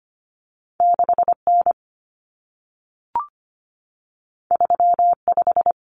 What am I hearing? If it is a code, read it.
Morse code, heard as a keyed tone, about 25 wpm, 700 Hz. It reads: 6N 35
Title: Common Cut Numbers - 2-Digit 25wpm